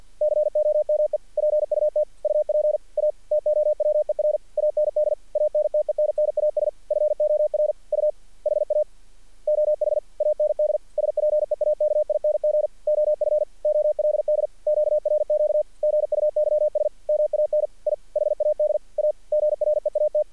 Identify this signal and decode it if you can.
Morse code, heard as a keyed tone, about 48 wpm, 600 Hz. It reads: COME PUT UP A TOWER AND ANTENNAS FOR U HA OH AND SPEAKING OF OLD QRQ GUYS MAN I HAD A GREAT
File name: archived / CRSnet